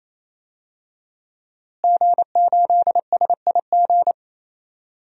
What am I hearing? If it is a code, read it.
Morse code, heard as a keyed tone, about 27 wpm, 700 Hz. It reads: G8SIG
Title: DXCC-100 Call Signs - 5 Characters - Repeated 3x - 28wpm